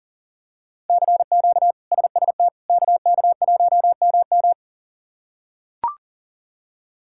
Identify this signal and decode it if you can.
Morse code, heard as a keyed tone, about 37 wpm, 700 Hz. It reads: CQ SST KK1MM
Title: K1USN SST QSO Practice - 40wpm